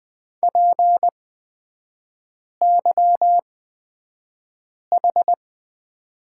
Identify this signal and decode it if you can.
Morse code, heard as a keyed tone, about 20 wpm, spaced as for 5 wpm, 690 Hz. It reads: PYH
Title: 229 Lesson 37e - Letter x - Three Characters 20wpm